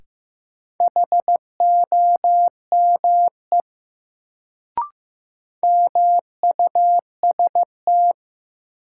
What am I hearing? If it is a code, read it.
Morse code, heard as a keyed tone, about 15 wpm, 700 Hz. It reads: HOME MUST
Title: Sentences from Top 200 Words - New Word Drill - 15wpm